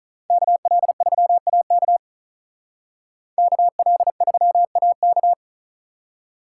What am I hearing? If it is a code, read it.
Morse code, heard as a keyed tone, about 45 wpm, 700 Hz. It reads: KL3AK KL3AK